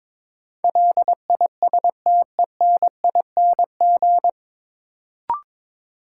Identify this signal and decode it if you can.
Morse code, heard as a keyed tone, about 22 wpm, 700 Hz. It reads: LISTENING
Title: Sets of 5 Words - New Word Drill - 22wpm